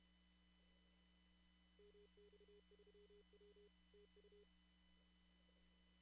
Morse code, heard as a keyed tone, about 31 wpm, 410 Hz. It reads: MX3W TU
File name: CW / kt5j wpx cw-op-k5pi